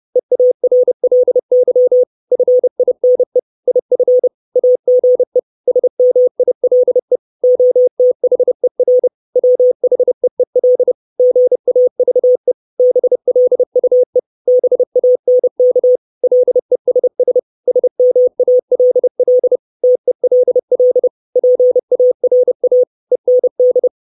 Morse code, heard as a keyed tone, about 30 wpm, 500 Hz. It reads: EARLY FINE IF AGE SMILE OTHER WHEEL GAVE BLUE BANK LESS SMALL TELL PARA END
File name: 632 Words 30WPM